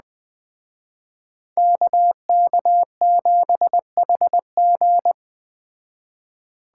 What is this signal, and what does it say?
Morse code, heard as a keyed tone, about 20 wpm, 700 Hz. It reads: KK7HG